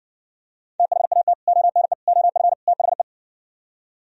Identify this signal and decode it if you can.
Morse code, heard as a keyed tone, about 65 wpm, 700 Hz. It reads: THAT ONE OF THE